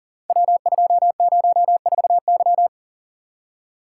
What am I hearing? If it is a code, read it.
Morse code, heard as a keyed tone, about 38 wpm, 690 Hz. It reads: W204Y
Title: Code Groups - 5 Letter:Numbers - 40wpm